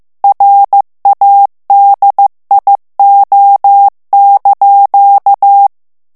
Morse code, heard as a keyed tone, about 15 wpm, 800 Hz. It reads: RADIO)
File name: ITALIEN texte clair 1 vitesse 14 mots minute